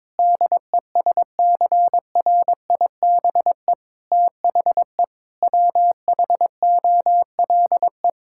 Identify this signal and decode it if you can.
Morse code, heard as a keyed tone, about 22 wpm, 700 Hz. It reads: DESCRIBE THE WHOLE
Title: Sentences from Top 1600 Words 22wpm